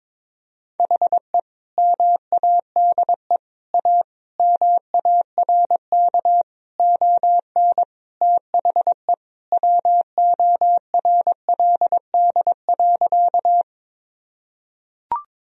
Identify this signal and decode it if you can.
Morse code, heard as a keyed tone, about 21 wpm, 700 Hz. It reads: HE MADE A MARK ON THE WORLD.
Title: Sentences from Top 1000 Words - Review Part 1 - 22wpm